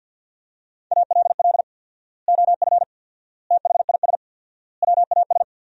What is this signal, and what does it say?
Morse code, heard as a keyed tone, about 50 wpm, 700 Hz. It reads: ALL OF THIS WAS
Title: 4x Word-Spacing - Sentences from Top 500 Words - 50wpm